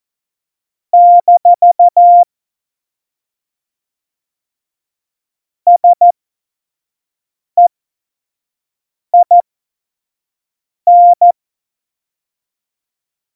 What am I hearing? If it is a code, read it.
Morse code, heard as a keyed tone, about 14 wpm, spaced as for 5 wpm, 700 Hz. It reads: - SEIN
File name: Morsekurs / Witze / Witz 03. November 2020, 25 F70